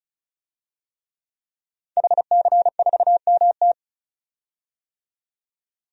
Morse code, heard as a keyed tone, about 36 wpm, 700 Hz. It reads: HC4MT
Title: DXCC-100 Call Signs - 5 Characters - 35wpm